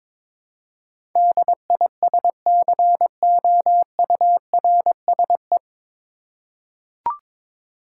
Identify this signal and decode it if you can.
Morse code, heard as a keyed tone, about 22 wpm, 700 Hz. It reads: DISCOURSE
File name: 9 Letter Words - First 3 Letters Given - 22wpm